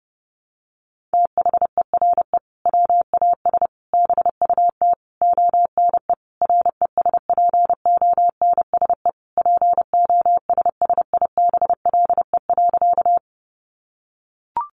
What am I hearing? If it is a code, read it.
Morse code, heard as a keyed tone, about 30 wpm, 700 Hz. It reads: THERE WAS BUT ONE RESPONSE POSSIBLE.
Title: Sentences from Top 1400 Words 30wpm